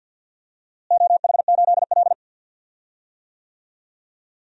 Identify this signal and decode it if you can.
Morse code, heard as a keyed tone, about 53 wpm, 700 Hz. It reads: OH8L
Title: DXCC-100 Call Signs - 4 Characters - Repeated 2x - 50wpm